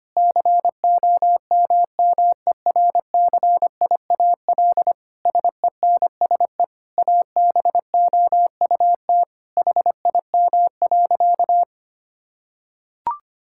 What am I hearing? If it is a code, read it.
Morse code, heard as a keyed tone, about 25 wpm, 700 Hz. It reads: COMMERCIAL SENSE ABOUT HIM.
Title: Sentences from Top 1000 Words - Review Part 5 - 25wpm